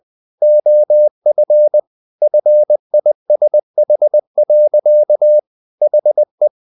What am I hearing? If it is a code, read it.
Morse code, heard as a keyed tone, about 20 wpm, 600 Hz. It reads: OF FISH. HE
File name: the old man and the sea / the old man and the sea part 1 20wpm